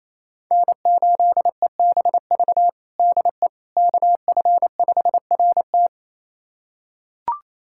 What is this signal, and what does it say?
Morse code, heard as a keyed tone, about 27 wpm, 700 Hz. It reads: N8EBV DE KF5RT
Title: USA Callsign DE Callsign - 5 Characters - 28wpm